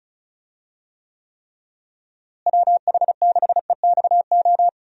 Morse code, heard as a keyed tone, about 34 wpm, 700 Hz. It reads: WH6EXO
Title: USA Call Signs - 6 Characters - Repeated 3x - 35wpm